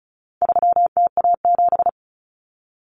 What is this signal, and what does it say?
Morse code, heard as a keyed tone, about 34 wpm, 700 Hz. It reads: 3TA7
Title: Common Cut Numbers - 4-Digit 35wpm